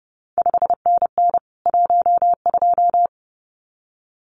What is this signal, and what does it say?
Morse code, heard as a keyed tone, about 30 wpm, 700 Hz. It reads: 5NN 12
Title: CQ WW DX Contest Exchange Encore - 30wpm